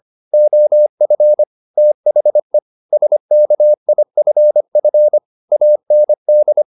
Morse code, heard as a keyed tone, about 25 wpm, 600 Hz. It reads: OF THE SKIFF AND
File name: the old man and the sea part 1 25wpm